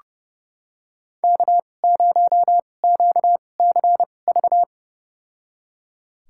Morse code, heard as a keyed tone, about 30 wpm, spaced as for 22 wpm, 700 Hz. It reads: K0QCV
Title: USA Call Signs - 5 Characters - 2x Character Spacing - 30wpm